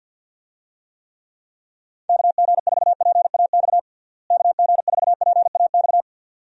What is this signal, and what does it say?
Morse code, heard as a keyed tone, about 52 wpm, 700 Hz. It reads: KG4PAX KG4PAX
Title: USA Call Signs - 6 Characters - Repeated 2x - 50wpm